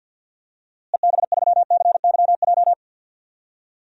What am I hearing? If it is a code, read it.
Morse code, heard as a keyed tone, about 50 wpm, 700 Hz. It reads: EB3CYJ
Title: DXCC-100 Call Signs - 6 Characters - Repeated 2x - 50wpm